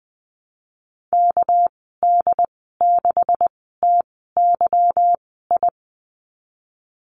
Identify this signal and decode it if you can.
Morse code, heard as a keyed tone, about 19 wpm, spaced as for 14 wpm, 700 Hz. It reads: KD6TYI
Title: 200 Lesson 32g - Letter k - Call Signs 20wpm